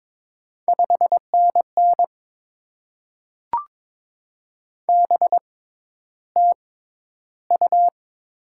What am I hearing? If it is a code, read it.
Morse code, heard as a keyed tone, about 22 wpm, spaced as for 11 wpm, 700 Hz. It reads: 5NN BTU
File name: Top 20 QSO Elements - 6x Character Spacing - 22wpm